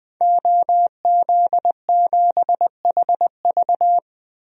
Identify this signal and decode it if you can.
Morse code, heard as a keyed tone, about 20 wpm, 700 Hz. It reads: OZ7HV
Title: DXCC-100 Call Signs - 5 Characters - 20wpm